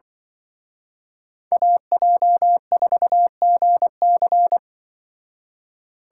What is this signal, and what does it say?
Morse code, heard as a keyed tone, about 24 wpm, 700 Hz. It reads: AJ4GC
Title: Speed Racing - USA 2x2 Call Signs 20wpm